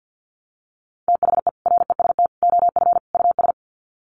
Morse code, heard as a keyed tone, about 50 wpm, 700 Hz. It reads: THE REST OF US
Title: Sets of 4 Words 50wpm